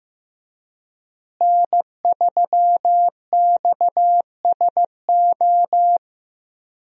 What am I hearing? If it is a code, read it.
Morse code, heard as a keyed tone, about 15 wpm, 700 Hz. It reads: N3XSO
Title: USA Call Signs - 5 Characters - Repeated 3x - 15wpm